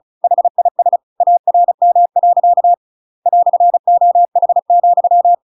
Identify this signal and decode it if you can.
Morse code, heard as a keyed tone, about 35 wpm, 700 Hz. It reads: HIS ARM. "OH,
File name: Pinocchio / Pinocchio ch7 35 35 ik1hge